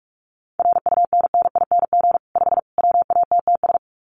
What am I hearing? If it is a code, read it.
Morse code, heard as a keyed tone, about 44 wpm, 700 Hz. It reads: RUNNING 5 WATTS
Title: Common QSO Phrases 45wpm